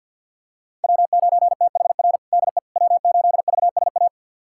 Morse code, heard as a keyed tone, about 51 wpm, 700 Hz. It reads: W9THR DE W8VSA
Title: USA Callsign DE Callsign - 5 Characters - 50wpm